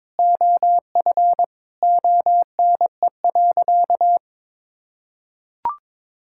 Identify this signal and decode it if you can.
Morse code, heard as a keyed tone, about 21 wpm, 700 Hz. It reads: OF ONE.
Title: Sentences from Top 1000 Words - Review Part 5 - 22wpm